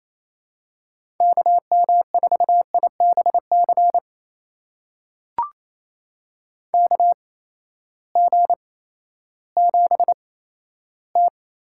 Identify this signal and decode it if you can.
Morse code, heard as a keyed tone, about 28 wpm, spaced as for 14 wpm, 700 Hz. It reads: KM4IBC KG7T
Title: USA Call Signs - 6 Characters - 8x Character Spacing - 28wpm